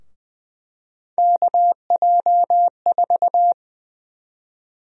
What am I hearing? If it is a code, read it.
Morse code, heard as a keyed tone, about 20 wpm, 700 Hz. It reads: KJ4
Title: USA Call Sign Prefixes - 20wpm